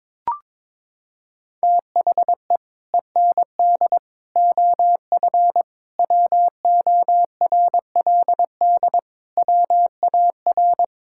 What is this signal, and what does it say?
Morse code, heard as a keyed tone, about 21 wpm, 700 Hz. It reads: THE END OF WORLD WAR
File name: practice / 22wpm / Sets of 6 Words - 22wpm